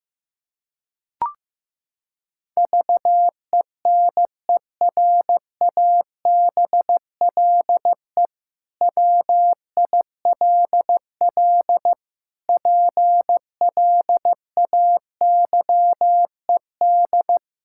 Morse code, heard as a keyed tone, about 16 wpm, 700 Hz. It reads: VENERABLE WILL PLAYED
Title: Pangram Challenge v2 - 15wpm